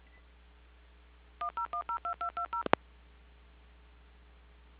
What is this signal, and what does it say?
DTMF, heard as telephone touch tones, tones of about 110 ms. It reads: 10102220
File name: recordings / PortlandUnion Millennium